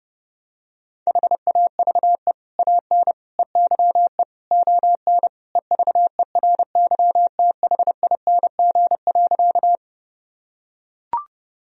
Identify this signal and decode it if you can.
Morse code, heard as a keyed tone, about 30 wpm, 700 Hz. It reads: HAVE AN EYE ON EVERYTHING.
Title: Sentences from Top 1000 Words - Review Part 1 - 30wpm